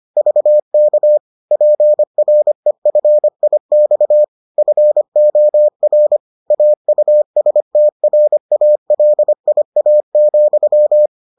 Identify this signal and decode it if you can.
Morse code, heard as a keyed tone, about 25 wpm, 600 Hz. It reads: VK PREFIX FOR AUSTRALIA,